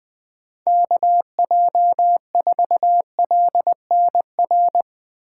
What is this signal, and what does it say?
Morse code, heard as a keyed tone, about 20 wpm, 700 Hz. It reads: KJ4LNR